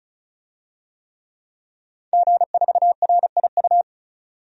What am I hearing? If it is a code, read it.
Morse code, heard as a keyed tone, about 36 wpm, 700 Hz. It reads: G4RIU
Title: DXCC-100 Call Signs - 5 Characters - 35wpm